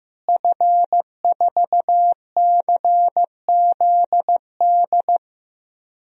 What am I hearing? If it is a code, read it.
Morse code, heard as a keyed tone, about 15 wpm, 700 Hz. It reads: F4CZD
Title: DXCC-100 Call Signs - 5 Characters - Repeated 2x - 15wpm